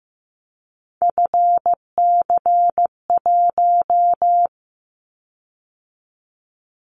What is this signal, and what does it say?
Morse code, heard as a keyed tone, about 15 wpm, 700 Hz. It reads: FC1
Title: Callsign Prefixes for Top 50 DXCC - 15wpm